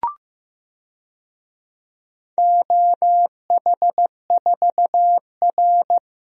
Morse code, heard as a keyed tone, about 15 wpm, 700 Hz. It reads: OH4R
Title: DXCC-100 Call Signs - 4 Characters - Repeated 3x - 15wpm